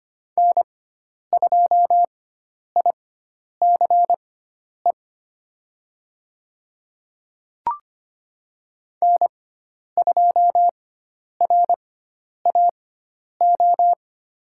Morse code, heard as a keyed tone, about 25 wpm, spaced as for 10 wpm, 700 Hz. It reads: N2ICE N2RAO
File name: Lesson 16g - Letter c - Call Signs 25wpm